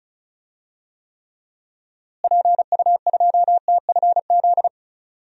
Morse code, heard as a keyed tone, about 36 wpm, 700 Hz. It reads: PU2TFZ
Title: DXCC-100 Call Signs - 6 Characters - Repeated 2x - 35wpm